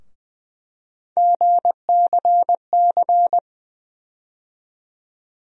Morse code, heard as a keyed tone, about 20 wpm, 700 Hz. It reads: GCC
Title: USA Call Sign Postfixes - 3 Letters 20wpm